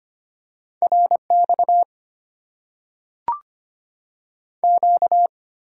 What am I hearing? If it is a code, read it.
Morse code, heard as a keyed tone, about 25 wpm, 690 Hz. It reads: RX Q
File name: Top 90 QSO Elements - 8x Character Spacing - 25wpm